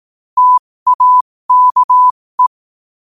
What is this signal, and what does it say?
Morse code, heard as a keyed tone, about 18 wpm, spaced as for 15 wpm, 1000 Hz. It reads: TAKE
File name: qotdfiles / 15wpm